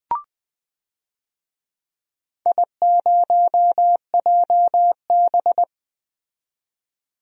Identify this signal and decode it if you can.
Morse code, heard as a keyed tone, about 20 wpm, 700 Hz. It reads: I0JB
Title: DXCC-100 Call Signs - 4 Characters - 20wpm